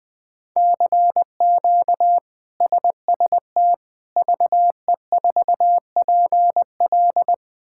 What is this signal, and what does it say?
Morse code, heard as a keyed tone, about 21 wpm, 700 Hz. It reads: CQ SST VE4PL
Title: K1USN SST QSO Practice - 20wpm